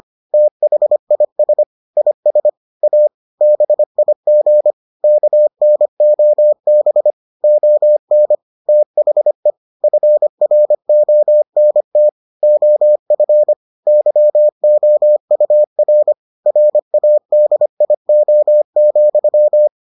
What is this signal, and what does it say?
Morse code, heard as a keyed tone, about 25 wpm, 600 Hz. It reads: THIS IS A BIG KNOB ON THE FRONT OF YOUR RADIO,